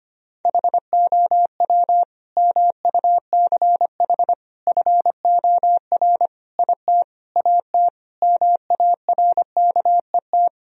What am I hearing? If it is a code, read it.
Morse code, heard as a keyed tone, about 25 wpm, 700 Hz. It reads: HOW MUCH FOR IT AT MARKET
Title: Sentences from Top 1000 Words - Review Part 3 - 25wpm